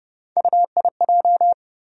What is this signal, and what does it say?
Morse code, heard as a keyed tone, about 30 wpm, 700 Hz. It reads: UIJ
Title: USA Call Sign Postfixes - 3 Letters 30wpm